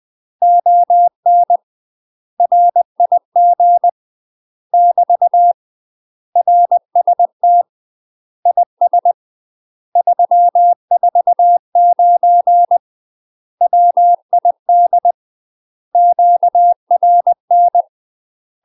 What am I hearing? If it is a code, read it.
Morse code, heard as a keyed tone, about 20 wpm, 700 Hz. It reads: ON RIG <BT> RST IS 349 WID QRN
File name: UK MASTER QSOs TRAINING / 20WPM / UK MASTER QSOs 0003